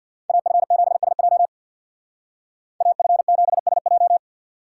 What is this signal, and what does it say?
Morse code, heard as a keyed tone, about 64 wpm, 700 Hz. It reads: AF7SJ AF7SJ